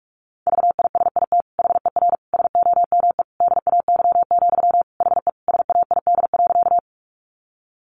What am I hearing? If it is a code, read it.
Morse code, heard as a keyed tone, about 44 wpm, 700 Hz. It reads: VISIT HER SOME DAY, HE SAID.
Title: Sentences from Top 900 Words 45wpm